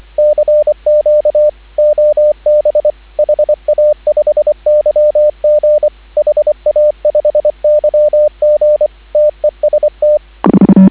Morse code, heard as a keyed tone, about 25 wpm, 590 Hz, with radio noise. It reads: CQ OB HA5YG HA5YG TEST